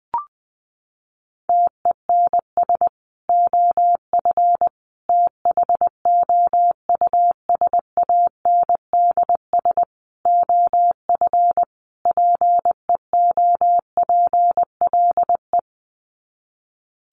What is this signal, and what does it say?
Morse code, heard as a keyed tone, about 20 wpm, 700 Hz. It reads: TENS OF THOUSANDS OF PEOPLE
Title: Sets of 5 Words Encore2 20wpm